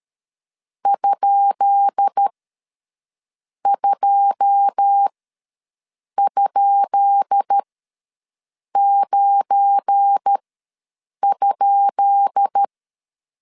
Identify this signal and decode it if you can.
Morse code, heard as a keyed tone, about 13 wpm, spaced as for 6 wpm, 800 Hz. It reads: ?2?9?
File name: LEZIONI / 10 SIGNS / 10B INT ws 12wpm 3527